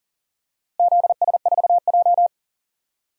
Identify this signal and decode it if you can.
Morse code, heard as a keyed tone, about 39 wpm, 700 Hz. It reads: ZS4J
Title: DXCC-100 Call Signs - 4 Characters - Repeated 3x - 40wpm